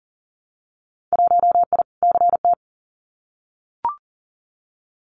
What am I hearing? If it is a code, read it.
Morse code, heard as a keyed tone, about 40 wpm, 700 Hz. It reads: 1I CT
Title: Winter Field Day Exchanges 40wpm